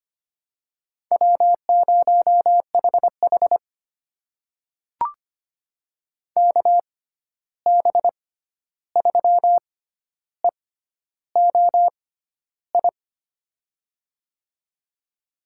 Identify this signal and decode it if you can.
Morse code, heard as a keyed tone, about 25 wpm, spaced as for 11 wpm, 700 Hz. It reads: W0HH KB3EOI
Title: USA Call Signs - 6x Character Spacing - 25wpm